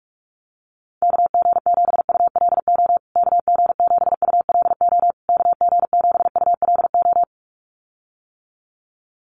Morse code, heard as a keyed tone, about 44 wpm, 690 Hz. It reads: KG7ULO KG7ULO KG7ULO
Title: USA Call Signs Repeated 3x 45wpm